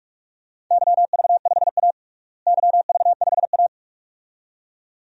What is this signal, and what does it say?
Morse code, heard as a keyed tone, about 46 wpm, 700 Hz. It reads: YV5A YV5A
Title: DXCC-100 Call Signs - 4 Characters - Repeated 2x - 45wpm